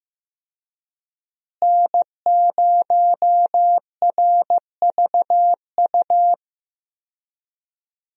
Morse code, heard as a keyed tone, about 16 wpm, 700 Hz. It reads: N0RVU
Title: USA Call Signs - 5 Characters - Repeated 3x - 15wpm